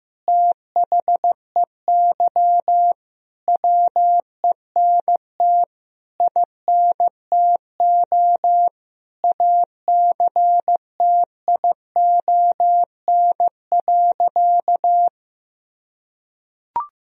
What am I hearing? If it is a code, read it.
Morse code, heard as a keyed tone, about 15 wpm, 700 Hz. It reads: THEY WENT INTO ACTION.
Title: Sentences from Top 1000 Words - Review Part 4 - 15wpm